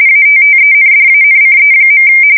RTTY, about 45.45 baud. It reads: ATTENDANCE.
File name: fsk170 / printing-telegraph